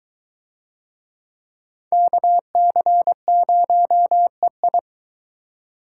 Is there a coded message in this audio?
Yes: Morse code from 1.9 to 4.8 s, heard as a keyed tone, about 22 wpm, 700 Hz. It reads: KC0EI